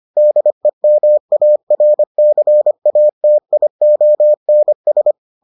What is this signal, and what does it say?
Morse code, heard as a keyed tone, about 25 wpm, 600 Hz. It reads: DEMARCATIONS